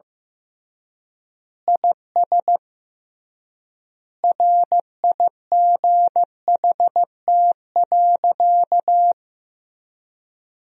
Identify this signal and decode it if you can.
Morse code, heard as a keyed tone, about 15 wpm, 700 Hz. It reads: IS RIGHT.
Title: 3x Word-Spacing - Sentences from Top 300 Words - 15wpm